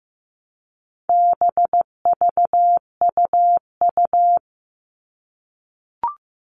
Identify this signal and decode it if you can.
Morse code, heard as a keyed tone, about 15 wpm, 700 Hz. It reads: BVUU
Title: All Cut Numbers - 4-Digit 15wpm